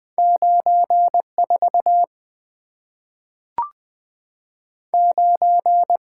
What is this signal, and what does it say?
Morse code, heard as a keyed tone, about 20 wpm, 700 Hz. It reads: 94 9
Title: Two Letter-Number Combinations - 20wpm